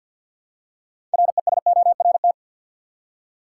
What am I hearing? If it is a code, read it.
Morse code, heard as a keyed tone, about 52 wpm, 700 Hz. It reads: RESORT
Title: Six Letter Words - 50wpm